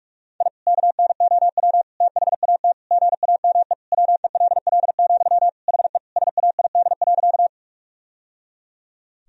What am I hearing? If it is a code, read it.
Morse code, heard as a keyed tone, about 41 wpm, 700 Hz. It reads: I KNOW THAT GAME WELL, HE SAID.
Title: Sentences from Top 1000 Words - Review Part 3 - 45wpm